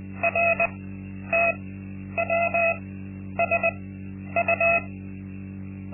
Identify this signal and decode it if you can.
Morse code, heard as a keyed tone, about 20 wpm, spaced as for 10 wpm, 680 Hz, with noise. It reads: RTWSU
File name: Linksmakalnis-2